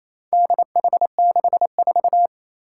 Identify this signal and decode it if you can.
Morse code, heard as a keyed tone, about 28 wpm, 700 Hz. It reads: DH64
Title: Four Challenging Letter-Number Combinations 28wpm